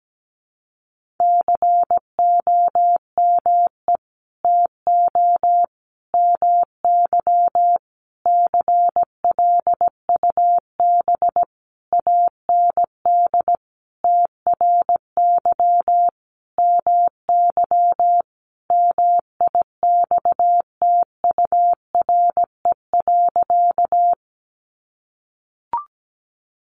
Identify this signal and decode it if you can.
Morse code, heard as a keyed tone, about 17 wpm, 700 Hz. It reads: COME TO MY CLUB AND TRY MY MIXTURE.
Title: Sentences from Top 1500 Words 17wpm